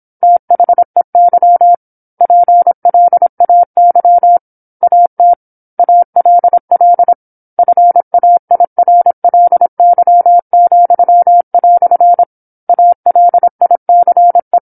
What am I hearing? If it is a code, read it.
Morse code, heard as a keyed tone, about 27 wpm, 700 Hz. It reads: THEY PLAY AT ALL FAIRLY," ALICE
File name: 08-Chapter-08 26 WPM